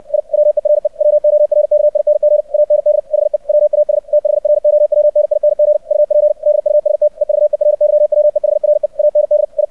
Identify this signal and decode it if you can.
Morse code, heard as a keyed tone, about 64 wpm, 600 Hz. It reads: I WERE WORKING AND HE WAS THROWNING UP DUST EVERYWEHRE AND I
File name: archived / CRSnet